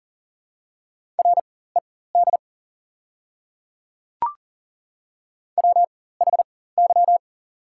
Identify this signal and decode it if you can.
Morse code, heard as a keyed tone, about 40 wpm, spaced as for 19 wpm, 690 Hz. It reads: RED WHY
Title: 4x Character Spacing - Top 500 Words - 40wpm